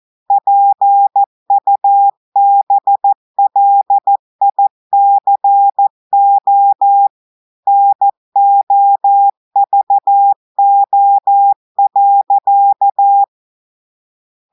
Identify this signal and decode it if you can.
Morse code, heard as a keyed tone, about 14 wpm, 800 Hz. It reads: PUBLICO NOVO.
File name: audios / portugais / portugais texte en clair 2 vitesse 14 mots minutes